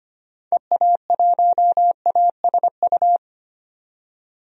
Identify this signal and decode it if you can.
Morse code, heard as a keyed tone, about 25 wpm, 700 Hz. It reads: EA1ASU
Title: DXCC-100 Call Signs - 6 Characters - 25wpm